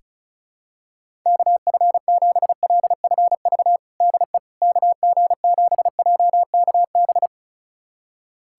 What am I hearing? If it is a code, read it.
Morse code, heard as a keyed tone, about 36 wpm, 700 Hz. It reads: KF7LFV DE KG7JKB
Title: USA Callsign DE Callsign 35wpm